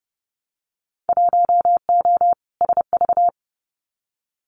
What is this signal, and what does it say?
Morse code, heard as a keyed tone, about 30 wpm, 700 Hz. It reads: 1O SV
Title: Winter Field Day Exchanges 30wpm